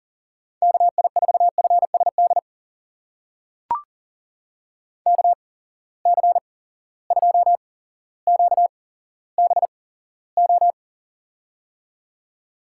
Morse code, heard as a keyed tone, about 37 wpm, spaced as for 17 wpm, 700 Hz. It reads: KI4FSD KC2QBO
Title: USA Call Signs - 6 Characters - 8x Character Spacing - 40wpm